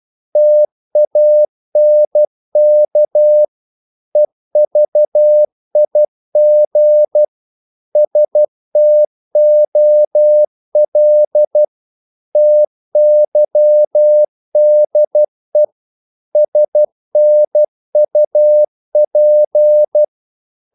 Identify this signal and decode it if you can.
Morse code, heard as a keyed tone, about 12 wpm, 600 Hz. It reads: TANK EVIG STOL TYDE SNUP
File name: Korte ord DK 12wpm